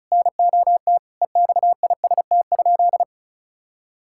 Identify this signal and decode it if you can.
Morse code, heard as a keyed tone, about 34 wpm, 700 Hz. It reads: NOT EXIST?
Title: Sentences from Top 1000 Words - Review Part 5 - 35wpm